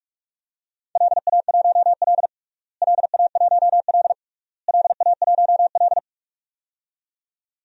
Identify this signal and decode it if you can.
Morse code, heard as a keyed tone, about 47 wpm, 700 Hz. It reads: LA1L LA1L LA1L
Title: DXCC-100 Call Signs - 4 Characters - Repeated 3x - 45wpm